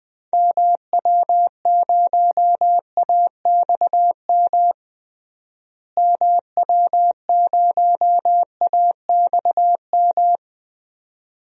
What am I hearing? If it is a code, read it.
Morse code, heard as a keyed tone, about 20 wpm, 700 Hz. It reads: MW0AXM MW0AXM
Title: DXCC-100 Call Signs - 6 Characters - Repeated 2x - 20wpm